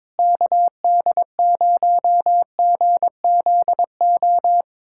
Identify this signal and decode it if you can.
Morse code, heard as a keyed tone, about 21 wpm, 700 Hz. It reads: KD0GZO
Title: USA Callsign DE Callsign 22wpm